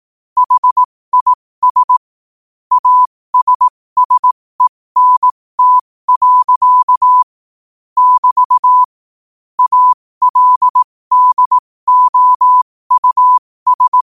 Morse code, heard as a keyed tone, about 18 wpm, spaced as for 15 wpm, 1000 Hz. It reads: HIS ASSENT. <BT> ALDOUS
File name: qotdfiles / 15wpm